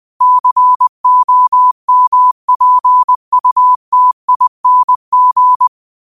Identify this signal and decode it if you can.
Morse code, heard as a keyed tone, about 20 wpm, 1000 Hz. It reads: COMPUTING